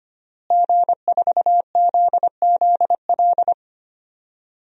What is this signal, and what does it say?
Morse code, heard as a keyed tone, about 25 wpm, 700 Hz. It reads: G4ZZL
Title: DXCC-100 Call Signs - 5 Characters - Repeated 3x - 25wpm